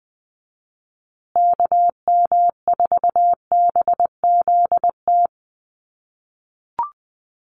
Morse code, heard as a keyed tone, about 20 wpm, 700 Hz. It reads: KM4BZT
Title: USA Call Signs Repeated 2x 20wpm